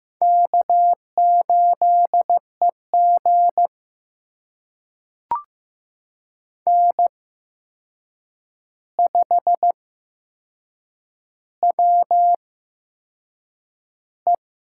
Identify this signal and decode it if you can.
Morse code, heard as a keyed tone, about 15 wpm, spaced as for 6 wpm, 700 Hz. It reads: K8EG N5WE
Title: USA Call Signs - 4 Characters - 8x Character Spacing - 15wpm